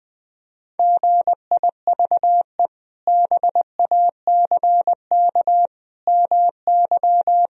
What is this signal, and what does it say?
Morse code, heard as a keyed tone, about 20 wpm, 700 Hz. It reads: GIVE BACK MY
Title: Sentences from Top 1000 Words - Review Part 1 - 20wpm